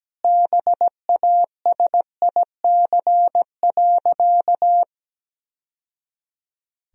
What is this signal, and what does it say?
Morse code, heard as a keyed tone, about 17 wpm, 700 Hz. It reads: BASIC.
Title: Sentences from Top 1000 Words - Review Part 3 - 17wpm